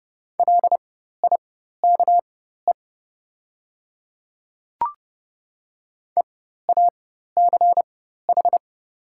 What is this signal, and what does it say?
Morse code, heard as a keyed tone, about 30 wpm, spaced as for 14 wpm, 700 Hz. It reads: LIKE EACH
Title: 4x Character Spacing - Top 200 Words - 30wpm